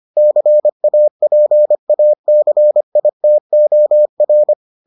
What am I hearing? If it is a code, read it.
Morse code, heard as a keyed tone, about 25 wpm, 600 Hz. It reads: CAPACITOR